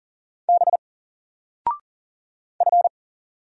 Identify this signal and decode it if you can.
Morse code, heard as a keyed tone, about 39 wpm, 700 Hz. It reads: B F
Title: Single Letter-Number 40wpm